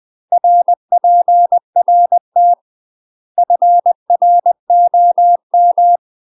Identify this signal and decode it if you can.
Morse code, heard as a keyed tone, about 20 wpm, 700 Hz. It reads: RPRT FROM
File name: UK MASTER QSOs TRAINING / 20WPM / UK MASTER QSOs 0055